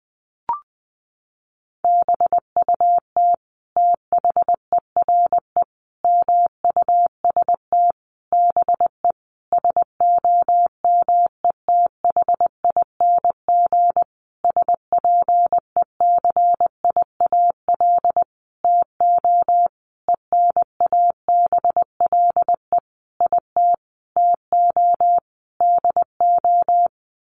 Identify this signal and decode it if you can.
Morse code, heard as a keyed tone, about 20 wpm, 700 Hz. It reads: BUT THERE MUST BE SOMETHING SPECIAL TO ENABLE IT TO DO
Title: Sentences from Top 1500 Words 20wpm